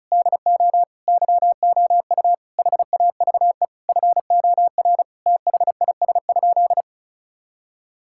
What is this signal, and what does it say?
Morse code, heard as a keyed tone, about 36 wpm, 700 Hz. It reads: DO YOU HAVE FOR THIS?
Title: Sentences from Top 1000 Words - Review Part 5 - 35wpm